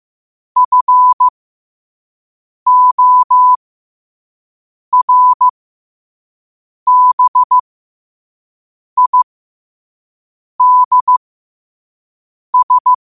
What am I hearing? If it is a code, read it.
Morse code, heard as a keyed tone, about 15 wpm, spaced as for 5 wpm, 1000 Hz. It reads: FORBIDS
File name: qotdfiles / 5wpm